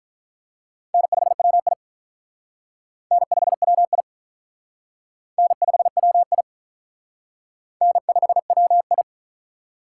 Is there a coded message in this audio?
Yes: Morse code from 0.9 to 9.0 s, heard as a keyed tone, about 45 wpm, 700 Hz. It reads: N5WI N5WI N5WI N5WI